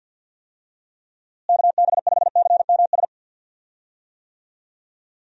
Morse code, heard as a keyed tone, about 48 wpm, 700 Hz. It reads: KB5CMS
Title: USA Call Signs - 6 Characters - Repeated 3x - 50wpm